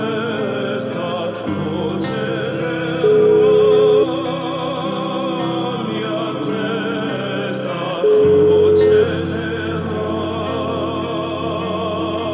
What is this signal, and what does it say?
Ringback tone (Europe, 425 Hz), with noise.